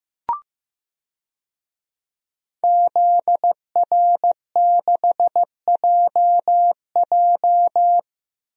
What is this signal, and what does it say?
Morse code, heard as a keyed tone, about 15 wpm, 700 Hz. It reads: ZR6JJ
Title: DXCC-100 Call Signs - 5 Characters - 15wpm